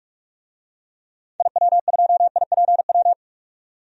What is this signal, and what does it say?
Morse code, heard as a keyed tone, about 45 wpm, 700 Hz. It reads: IW2IPW
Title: DXCC-100 Call Signs - 6 Characters - 45wpm